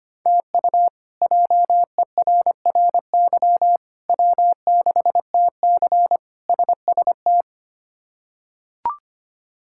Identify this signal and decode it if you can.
Morse code, heard as a keyed tone, about 25 wpm, 700 Hz. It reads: TU JERRY W6TC SST
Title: K1USN SST QSO Practice - 25wpm